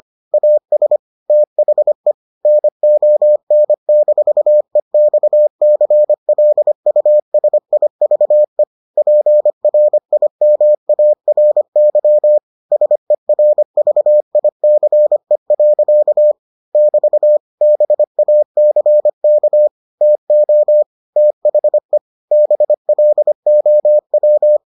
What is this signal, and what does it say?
Morse code, heard as a keyed tone, about 25 wpm, 600 Hz. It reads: AS THE NON-EXCLUSIVE PRIMARY SERVICE. <BT> BACK TO THE BLOW